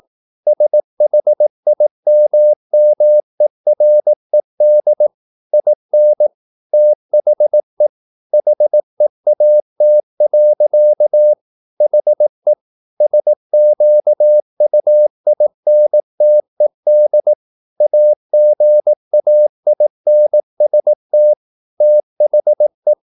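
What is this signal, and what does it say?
Morse code, heard as a keyed tone, about 18 wpm, 600 Hz. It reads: SHIMMERED IN THE HEAT. HE SQUINTED AGAINST THE
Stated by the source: Morse practice at 18 wpm, English text, from CW med Gnister.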